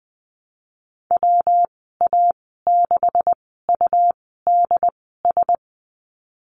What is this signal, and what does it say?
Morse code, heard as a keyed tone, about 20 wpm, spaced as for 15 wpm, 700 Hz. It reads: WA6UDS
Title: 200 Lesson 32g - Letter k - Call Signs 20wpm